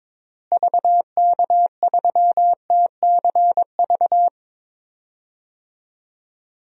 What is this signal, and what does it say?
Morse code, heard as a keyed tone, about 22 wpm, 700 Hz. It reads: VK3TCV
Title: DXCC-100 Call Signs - 6 Characters - Repeated 2x - 22wpm